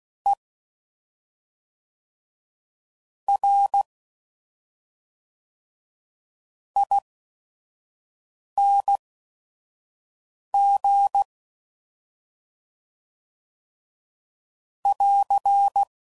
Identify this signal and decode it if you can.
Morse code, heard as a keyed tone, about 16 wpm, spaced as for 5 wpm, 800 Hz. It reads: E R ING <AR>
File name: Suffixes / Special ING's / SlowToFaster ING's